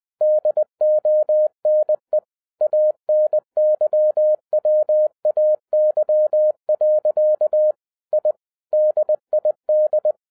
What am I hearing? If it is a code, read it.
Morse code, heard as a keyed tone, about 20 wpm, 610 Hz. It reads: DONE ANYWAY. I DID
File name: 2011-07-mp3 / n58 BACK FROM A BIG ACCIDENT 20wpm